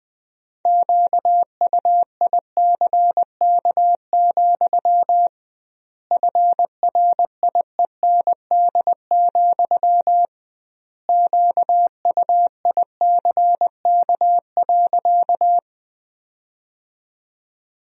Morse code, heard as a keyed tone, about 21 wpm, 700 Hz. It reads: QUICK, FRIEND, QUICK.
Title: 2x Word-Spacing - Sentences from Top 500 Words - 20wpm